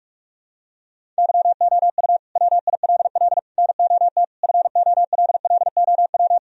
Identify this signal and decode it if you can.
Morse code, heard as a keyed tone, about 43 wpm, 700 Hz. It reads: YOU WILL NOT FOLLOW
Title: Sentences from Top 1000 Words - Review Part 1 - 45wpm